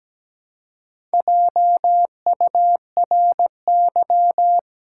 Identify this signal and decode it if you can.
Morse code, heard as a keyed tone, about 17 wpm, 700 Hz. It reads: JURY
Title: Four Letter Words - 17wpm